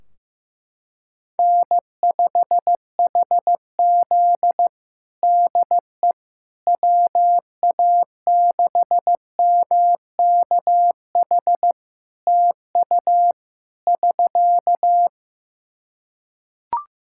Morse code, heard as a keyed tone, about 15 wpm, 700 Hz. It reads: N5HZ DE WA6MKH TU <SK>
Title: Practice QSO - Exchange 4 - Ending - 15wpm